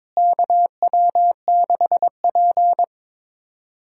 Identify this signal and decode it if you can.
Morse code, heard as a keyed tone, about 22 wpm, 700 Hz. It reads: KW6P
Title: USA Call Signs - 4 Characters - Repeated 2x - 22wpm